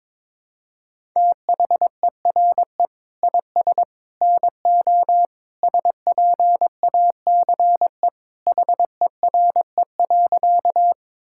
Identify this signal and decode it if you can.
Morse code, heard as a keyed tone, about 22 wpm, 700 Hz. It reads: THERE IS NO SPACE HERE.
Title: Sentences from Top 1000 Words - Review Part 5 - 22wpm